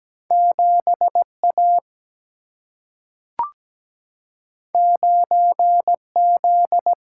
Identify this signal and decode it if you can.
Morse code, heard as a keyed tone, about 17 wpm, 700 Hz. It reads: 7 A 9 Z
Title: Single-Letter Single-Number 17wpm